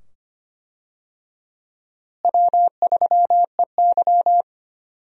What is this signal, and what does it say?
Morse code, heard as a keyed tone, about 25 wpm, 700 Hz. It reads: W3EY
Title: USA Call Signs - 4 Characters - Repeated 2x - 25wpm